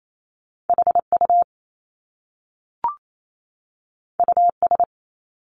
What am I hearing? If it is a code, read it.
Morse code, heard as a keyed tone, about 28 wpm, 700 Hz. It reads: HU US
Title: Top Two Letter Combinations in Words 28wpm